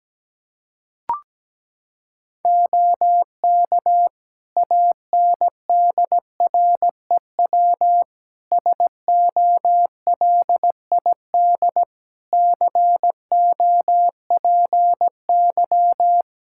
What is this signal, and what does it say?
Morse code, heard as a keyed tone, about 17 wpm, 700 Hz. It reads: OK ANDREW SOLID COPY
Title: Practice QSO - Exchange 3 - More Info - 17wpm